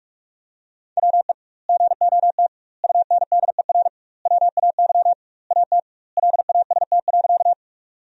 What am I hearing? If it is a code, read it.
Morse code, heard as a keyed tone, about 46 wpm, 700 Hz. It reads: WE GOT UNDER WAY AT LAST.
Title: 2x Word-Spacing - Sentences from Top 300 Words - 45wpm